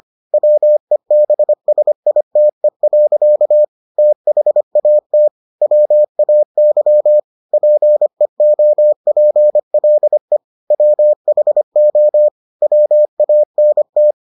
Morse code, heard as a keyed tone, about 25 wpm, 600 Hz. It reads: WEBSITE. THAT WAY PEOPLE WHO WANT